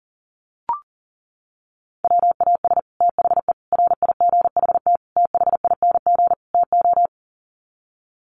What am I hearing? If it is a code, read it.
Morse code, heard as a keyed tone, about 40 wpm, 700 Hz. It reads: WAS THE RIGHT THING TO
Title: Sets of 5 Words Encore2 40wpm